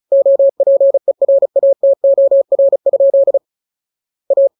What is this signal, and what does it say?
Morse code, heard as a keyed tone, about 34 wpm, 550 Hz. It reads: OPERATOR? A
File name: jokes / joke 24 35 0000